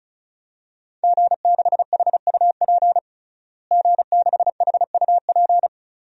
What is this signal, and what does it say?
Morse code, heard as a keyed tone, about 36 wpm, 700 Hz. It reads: G6HUP G6HUP
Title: DXCC-100 Call Signs - 5 Characters - Repeated 2x - 35wpm